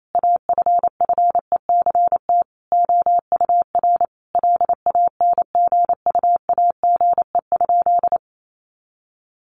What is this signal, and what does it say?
Morse code, heard as a keyed tone, about 28 wpm, 700 Hz. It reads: AFFECT OUR LANGUAGE?
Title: Sentences from Top 900 Words 28wpm